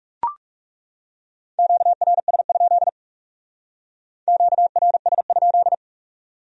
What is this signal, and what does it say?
Morse code, heard as a keyed tone, about 36 wpm, 700 Hz. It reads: QRS? QRS?
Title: Speed-Racing Q-Codes - 30wpm